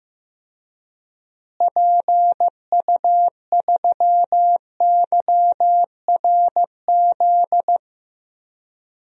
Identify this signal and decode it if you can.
Morse code, heard as a keyed tone, about 15 wpm, 700 Hz. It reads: PU3YRZ
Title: DXCC-100 Call Signs - 6 Characters - Repeated 3x - 15wpm